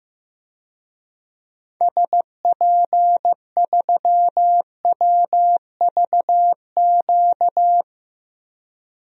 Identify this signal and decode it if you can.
Morse code, heard as a keyed tone, about 15 wpm, 700 Hz. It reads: SP3WVQ
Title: DXCC-100 Call Signs - 6 Characters - 15wpm